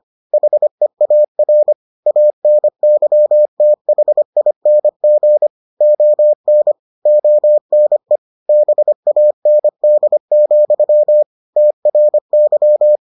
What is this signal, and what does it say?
Morse code, heard as a keyed tone, about 25 wpm, 600 Hz. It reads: HEAR ANYTHING ON ONE BAND, TRY